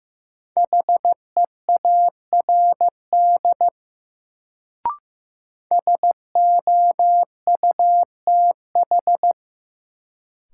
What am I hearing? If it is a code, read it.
Morse code, heard as a keyed tone, about 15 wpm, 700 Hz. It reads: HEARD SOUTH
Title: Sentences from Top 400 Words - New Word Drill - 15wpm